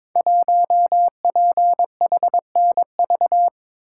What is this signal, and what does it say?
Morse code, heard as a keyed tone, about 22 wpm, 700 Hz. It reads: 1PHNV
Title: Code Groups - 5 Letter:Numbers - 22wpm